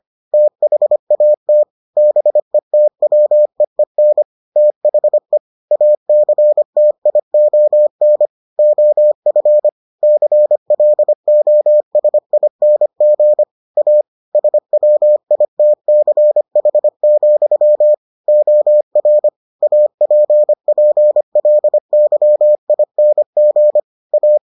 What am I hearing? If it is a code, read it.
Morse code, heard as a keyed tone, about 25 wpm, 600 Hz. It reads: THAT BETWEEN THE ACTION OF CLOSING A SWITCH, OR APPLYING A